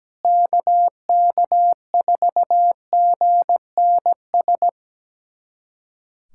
Morse code, heard as a keyed tone, about 17 wpm, 700 Hz. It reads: KK4GNS
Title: USA Callsign DE Callsign 17wpm